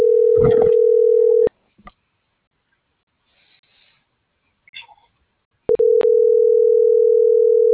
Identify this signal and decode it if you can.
Ringback tone (North America), 440 + 480 Hz.